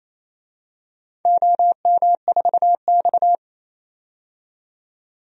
Morse code, heard as a keyed tone, about 29 wpm, 700 Hz. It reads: OM4X
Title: DXCC-100 Call Signs - 4 Characters - 28wpm